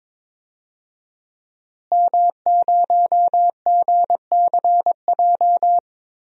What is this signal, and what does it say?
Morse code, heard as a keyed tone, about 22 wpm, 700 Hz. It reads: M0GCJ
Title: DXCC-100 Call Signs - 5 Characters - Repeated 2x - 22wpm